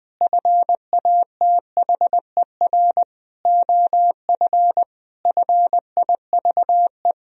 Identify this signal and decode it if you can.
Morse code, heard as a keyed tone, about 20 wpm, 700 Hz. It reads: FATHER OF FIVE
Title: Sentences from Top 1000 Words - Review Part 2 - 20wpm